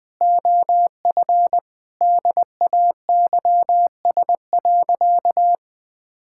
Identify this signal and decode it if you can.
Morse code, heard as a keyed tone, about 20 wpm, 700 Hz. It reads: OF DAYS.
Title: Sentences with 4-Word Phrases - 20wpm